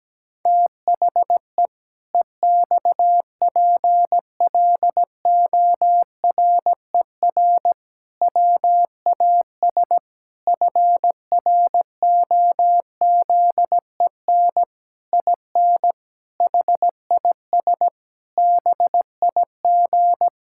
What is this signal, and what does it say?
Morse code, heard as a keyed tone, about 17 wpm, 700 Hz. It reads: THE EXPLORER WAS FROZEN IN HIS BIG
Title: Pangram Challenge v2 - 17wpm